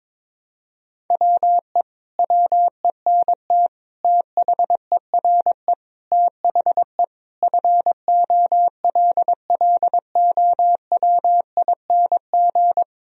Morse code, heard as a keyed tone, about 22 wpm, 700 Hz. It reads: WE WENT THERE THE FOLLOWING
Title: Sentences from Top 1000 Words - Review Part 4 - 22wpm